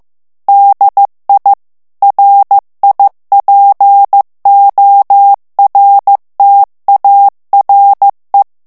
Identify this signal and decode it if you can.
Morse code, heard as a keyed tone, about 15 wpm, 800 Hz. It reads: DI RIPORTARE
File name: morse mp3 / ITALIEN texte clair 2 vitesse 14 mots minute